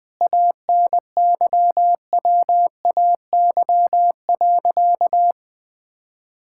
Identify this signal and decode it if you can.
Morse code, heard as a keyed tone, about 19 wpm, 700 Hz. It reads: ANYWAY.
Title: Sentences with 4-Word Phrases - Spoken Beginning - 20wpm